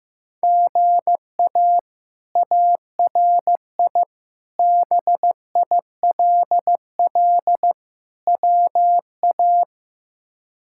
Morse code, heard as a keyed tone, about 16 wpm, 700 Hz. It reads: GA ARI BILL WA
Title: K1USN SST QSO Practice - 15wpm